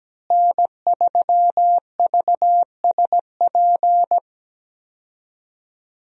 Morse code, heard as a keyed tone, about 17 wpm, 700 Hz. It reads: N3VSP
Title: USA Call Signs - 5 Characters - Repeated 2x - 17wpm